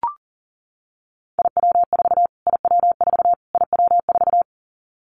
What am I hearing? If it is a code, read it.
Morse code, heard as a keyed tone, about 40 wpm, 700 Hz. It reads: IW4 IW4 IW4
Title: Callsign Prefixes for Top 25 DXCC - Repeated 3x - 40wpm